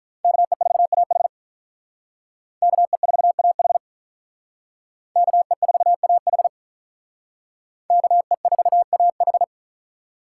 Morse code, heard as a keyed tone, about 43 wpm, 700 Hz. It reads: KE4AH KE4AH KE4AH KE4AH
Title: Speed Racing - USA 2x2 Call Signs 35wpm